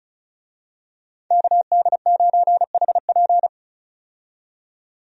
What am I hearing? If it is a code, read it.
Morse code, heard as a keyed tone, about 36 wpm, 700 Hz. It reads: KD9HP
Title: USA Call Signs - 5 Characters - 35wpm